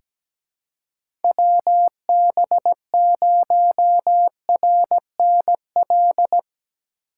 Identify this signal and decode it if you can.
Morse code, heard as a keyed tone, about 17 wpm, 700 Hz. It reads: WB0RNL
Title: USA Call Signs - 6 Characters - 3x Character Spacing - 17wpm